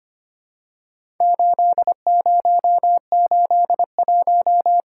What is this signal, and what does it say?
Morse code, heard as a keyed tone, about 25 wpm, 690 Hz. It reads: 8081
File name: practice / 25wpm / Four Letter-Number Combinations - 25wpm